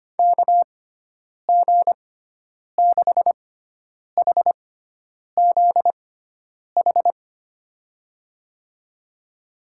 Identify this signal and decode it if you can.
Morse code, heard as a keyed tone, about 25 wpm, spaced as for 9 wpm, 700 Hz. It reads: KG6HZH
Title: USA Call Signs - 6 Characters - 6x Character Spacing - 25wpm